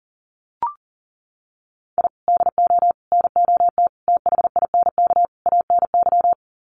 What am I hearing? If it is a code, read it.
Morse code, heard as a keyed tone, about 37 wpm, 700 Hz. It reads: I DO NOT THINK ANY
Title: Sets of 5 Words Encore 40wpm